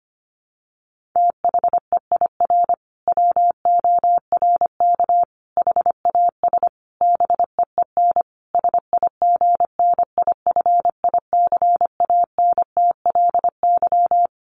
Morse code, heard as a keyed tone, about 25 wpm, 700 Hz. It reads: THEIR WORK HAS BEEN SIGNIFICANTLY
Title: Sentences from Top 1500 Words 25wpm